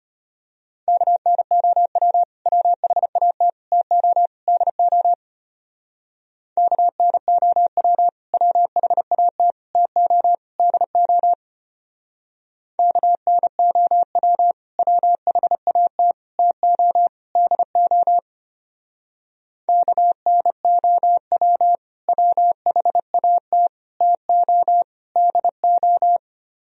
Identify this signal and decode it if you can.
Morse code, heard as a keyed tone, about 31 wpm, 700 Hz. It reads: KNOW WHAT TO DO KNOW WHAT TO DO KNOW WHAT TO DO KNOW WHAT TO DO
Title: Speed-Racing - Sets of 4 Words 25wpm